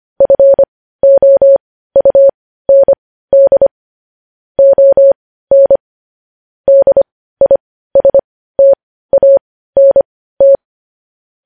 Morse code, heard as a keyed tone, about 25 wpm, spaced as for 15 wpm, 550 Hz. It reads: FOUND ON DISTANT
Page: news headlines in morse code at 15 wpm